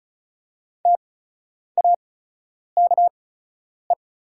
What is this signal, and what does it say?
Morse code, heard as a keyed tone, about 34 wpm, spaced as for 10 wpm, 700 Hz. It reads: TAKE
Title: 8x Character Spacing - Top 300 Words - 35wpm